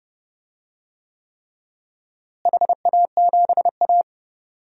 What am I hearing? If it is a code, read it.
Morse code, heard as a keyed tone, about 31 wpm, 700 Hz. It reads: HA7A
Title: DXCC-100 Call Signs - 4 Characters - Repeated 3x - 30wpm